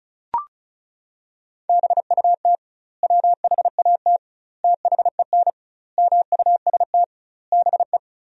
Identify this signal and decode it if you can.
Morse code, heard as a keyed tone, about 34 wpm, 700 Hz. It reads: BUT WHAT THEN MUST BE
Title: 2x Word-Spacing - Sentences from Top 500 Words - 35wpm